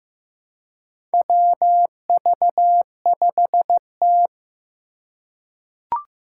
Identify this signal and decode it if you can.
Morse code, heard as a keyed tone, about 15 wpm, 700 Hz. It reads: WV5T
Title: Four Challenging Letter-Number Combinations 15wpm